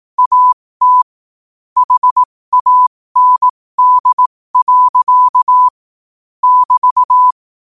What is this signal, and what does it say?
Morse code, heard as a keyed tone, about 18 wpm, spaced as for 15 wpm, 1000 Hz. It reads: AT HAND. <BT>
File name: qotdfiles / 15wpm